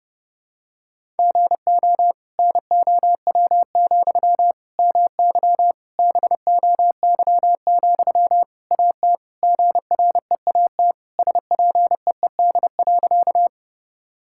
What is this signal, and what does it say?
Morse code, heard as a keyed tone, about 30 wpm, 700 Hz. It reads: GO NOW, MY BOY, AT GREAT SPEED.
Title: Sentences from Top 1000 Words - Review Part 5 - 30wpm